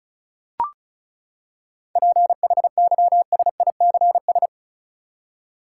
Morse code, heard as a keyed tone, about 34 wpm, 700 Hz. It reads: PHYSICS
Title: Seven Letter Words - 35wpm